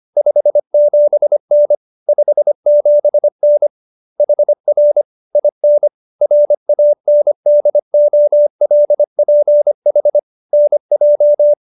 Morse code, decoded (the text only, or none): 57N 57N HR IN RANDOLPH NJ